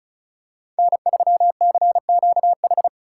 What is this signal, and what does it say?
Morse code, heard as a keyed tone, about 36 wpm, 700 Hz. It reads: N3CQH
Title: USA Call Signs - 5 Characters - 8x Character Spacing - 35wpm